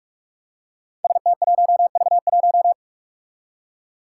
Morse code, heard as a keyed tone, about 46 wpm, 700 Hz. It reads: ST1V1
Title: Five Challenging Letter-Number Combinations 45wpm